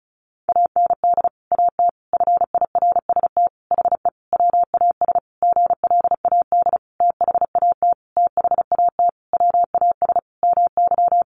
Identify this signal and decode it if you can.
Morse code, heard as a keyed tone, about 34 wpm, 700 Hz. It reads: AND AT FIRST HE WAS GLAD THAT THAT WAS MY
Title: Sentences from Top 1300 Words 35wpm